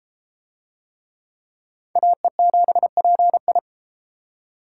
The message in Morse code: AE7PI

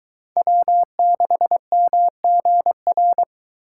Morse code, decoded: W6MGR

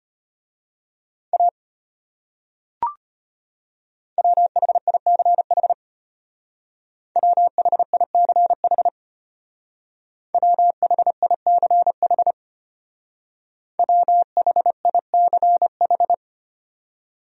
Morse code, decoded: A WHICH WHICH WHICH WHICH

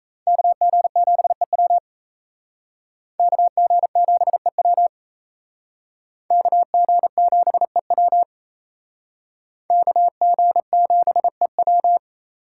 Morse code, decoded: KG7EW KG7EW KG7EW KG7EW

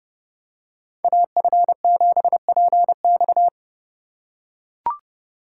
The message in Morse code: AF7PX

